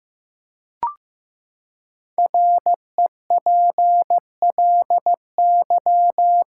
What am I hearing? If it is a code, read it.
Morse code, heard as a keyed tone, about 15 wpm, 700 Hz. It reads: REPLY
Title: Five Letter Words - 15wpm